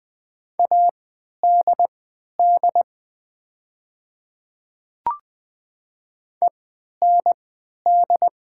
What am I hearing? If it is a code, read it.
Morse code, heard as a keyed tone, about 20 wpm, spaced as for 11 wpm, 700 Hz. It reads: ADD END
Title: 3x Character Spacing - Top 200 Words - 20wpm